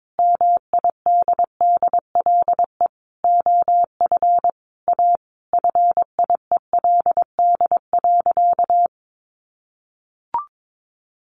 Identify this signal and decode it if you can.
Morse code, heard as a keyed tone, about 22 wpm, 700 Hz. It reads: MIDDLE OF A FIELD.
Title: Sentences from Top 1000 Words 22wpm